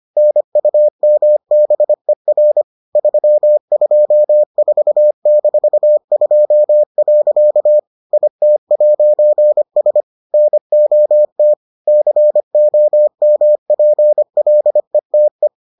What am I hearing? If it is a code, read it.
Morse code, heard as a keyed tone, about 25 wpm, 600 Hz. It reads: NUMBER 324-2. IT'S NOT COMPLETE